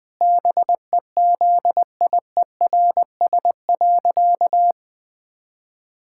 Morse code, heard as a keyed tone, about 19 wpm, 700 Hz. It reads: BEZIERS.
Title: Pangram Challenge v2 - 20wpm